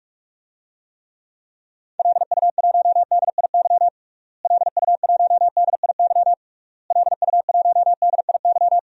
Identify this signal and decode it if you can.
Morse code, heard as a keyed tone, about 46 wpm, 700 Hz. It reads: LU1DIY LU1DIY LU1DIY
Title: DXCC-100 Call Signs - 6 Characters - Repeated 3x - 45wpm